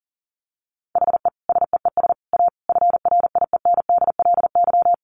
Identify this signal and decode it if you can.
Morse code, heard as a keyed tone, about 40 wpm, 700 Hz. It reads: HE SEES A FRIENDLY
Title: Easy Phrases 40wpm